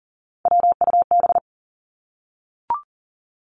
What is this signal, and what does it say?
Morse code, heard as a keyed tone, about 40 wpm, 700 Hz. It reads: WUB
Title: Three Challenging Letter-Number Combinations 40wpm